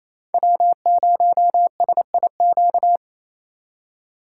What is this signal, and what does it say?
Morse code, heard as a keyed tone, about 35 wpm, 700 Hz. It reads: W0SIQ
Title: Speed Racing - USA 1x3 Call Signs 28wpm